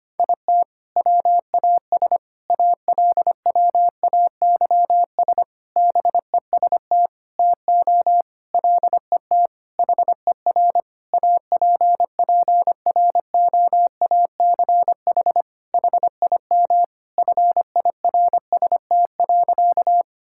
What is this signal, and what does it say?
Morse code, heard as a keyed tone, about 25 wpm, 700 Hz. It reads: IT WAS ALWAYS BEST TO LET HER APPROACH HIM FIRST.
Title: Sentences from Top 1000 Words - Review Part 6 - 25wpm